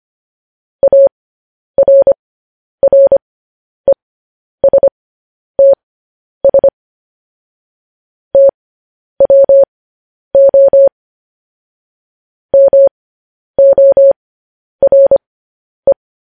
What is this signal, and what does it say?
Morse code, heard as a keyed tone, about 25 wpm, spaced as for 10 wpm, 550 Hz. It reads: ARRESTS TWO MORE